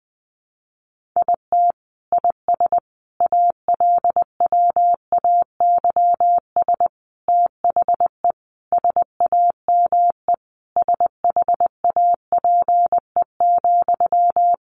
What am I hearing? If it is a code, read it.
Morse code, heard as a keyed tone, about 20 wpm, 700 Hz. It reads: IT IS ALWAYS THE SAME SHAPE,
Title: Sentences from Top 1400 Words 20wpm